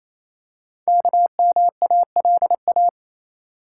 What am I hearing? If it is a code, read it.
Morse code, heard as a keyed tone, about 28 wpm, 700 Hz. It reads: KMALA
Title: Code Groups - 5 Letters - 28wpm